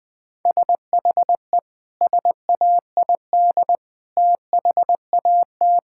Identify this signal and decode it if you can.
Morse code, heard as a keyed tone, about 20 wpm, 700 Hz. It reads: SHE SAID THAT
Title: Sentences with 3-Word Phrases - 20wpm